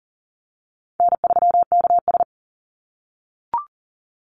Morse code, heard as a keyed tone, about 40 wpm, 700 Hz. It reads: N3KS
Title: Easy Call Signs - 40wpm